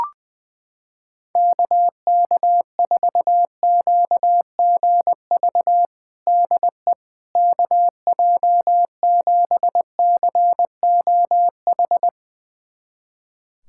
Morse code, heard as a keyed tone, about 20 wpm, 700 Hz. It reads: KK4QGV DE KJ7COH
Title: USA Callsign DE Callsign 20wpm